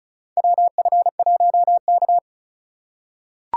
Morse code, heard as a keyed tone, about 36 wpm, 700 Hz. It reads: WF1K
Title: USA Call Signs - 4 Characters - 3x Character Spacing - 35wpm